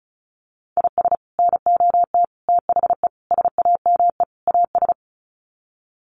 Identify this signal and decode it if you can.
Morse code, heard as a keyed tone, about 34 wpm, 700 Hz. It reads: IS NOT THE SAME AS
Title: Sets of 5 Words Encore 35wpm